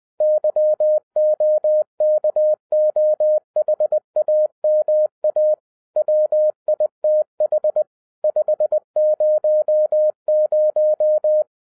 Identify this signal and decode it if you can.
Morse code, heard as a keyed tone, about 20 wpm, 610 Hz. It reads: YOKOHAMA WITH 500
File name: 2011-07-mp3 / n18 GREETINGS FROM SINGAPORE 20wpm